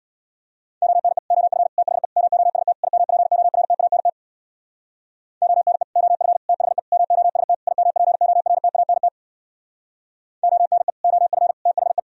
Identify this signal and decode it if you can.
Morse code, heard as a keyed tone, about 60 wpm, 700 Hz. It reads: ONE OF THE MOST IMPORTANT ONE OF THE MOST IMPORTANT ONE OF THE
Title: Speed-Racing - Sets of 5 Words 50wpm